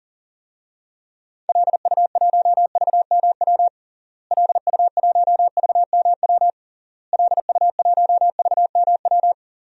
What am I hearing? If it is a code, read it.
Morse code, heard as a keyed tone, about 40 wpm, 700 Hz. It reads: LU1VMW LU1VMW LU1VMW
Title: DXCC-100 Call Signs - 6 Characters - Repeated 3x - 40wpm